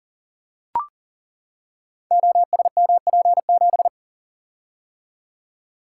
Morse code, heard as a keyed tone, about 38 wpm, 700 Hz. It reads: OSMP7
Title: Code Groups - 5 Letter:Numbers - 40wpm